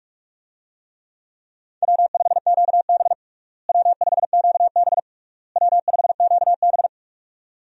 Morse code, heard as a keyed tone, about 46 wpm, 700 Hz. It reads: W5QB W5QB W5QB
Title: USA Call Signs - 4 Characters - Repeated 3x - 45wpm